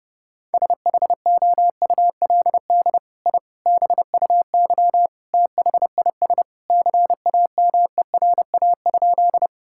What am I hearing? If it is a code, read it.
Morse code, heard as a keyed tone, about 32 wpm, 700 Hz. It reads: SHOULD I BUY THIS CAMERA?
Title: Sentences from Top 1000 Words - Review Part 5 - 30wpm